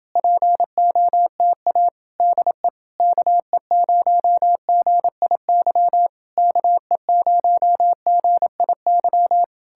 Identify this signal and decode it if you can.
Morse code, heard as a keyed tone, about 32 wpm, 700 Hz. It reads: POTA DE KE0GIY KE0GIY